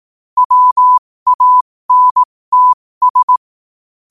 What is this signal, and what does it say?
Morse code, heard as a keyed tone, about 18 wpm, spaced as for 15 wpm, 1000 Hz. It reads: WANTS
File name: qotdfiles / 15wpm